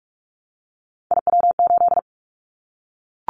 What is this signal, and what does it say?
Morse code, heard as a keyed tone, about 44 wpm, 700 Hz. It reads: IW8
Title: Callsign Prefixes for Top 25 DXCC - Repeated 3x - 45wpm